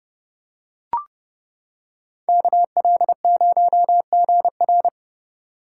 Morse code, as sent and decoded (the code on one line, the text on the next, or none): -.- .-.. ----- --. .-.
KL0GR